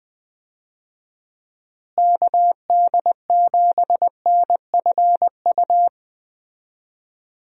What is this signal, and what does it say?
Morse code, heard as a keyed tone, about 20 wpm, 700 Hz. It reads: KD7NFU
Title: USA Call Signs - 6 Characters - Repeated 2x - 20wpm